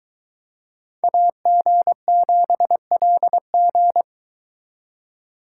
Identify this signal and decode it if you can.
Morse code, heard as a keyed tone, about 22 wpm, 700 Hz. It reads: AG7LG